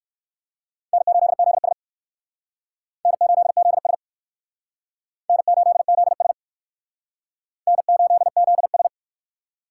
Morse code, heard as a keyed tone, about 59 wpm, 700 Hz. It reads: N8ZS N8ZS N8ZS N8ZS